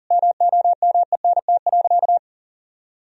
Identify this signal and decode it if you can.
Morse code, heard as a keyed tone, about 40 wpm, 700 Hz. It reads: MOMENT.
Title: Sentences from Top 1000 Words - Review Part 3 - 40wpm